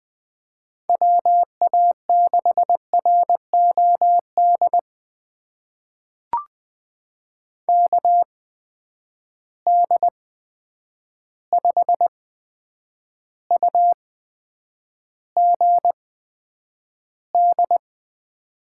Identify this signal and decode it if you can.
Morse code, heard as a keyed tone, about 20 wpm, spaced as for 9 wpm, 700 Hz. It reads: WA6ROD KD5UGD
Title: USA Call Signs - 6 Characters - 8x Character Spacing - 20wpm